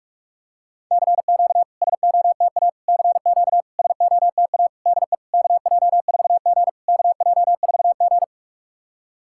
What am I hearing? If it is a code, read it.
Morse code, heard as a keyed tone, about 44 wpm, 690 Hz. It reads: CQ SOTA CQ SOTA DE KJ4G KJ4G
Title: SOTA QSO Practice 45wpm